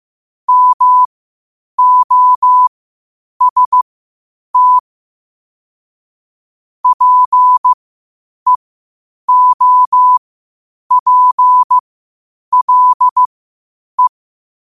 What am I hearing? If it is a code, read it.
Morse code, heard as a keyed tone, about 15 wpm, spaced as for 8 wpm, 1000 Hz. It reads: MOST PEOPLE
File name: qotdfiles / 8wpm